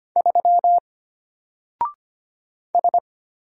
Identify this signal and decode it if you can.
Morse code, heard as a keyed tone, about 25 wpm, 700 Hz. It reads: 3 S
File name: Single-Letter-Number-25wpm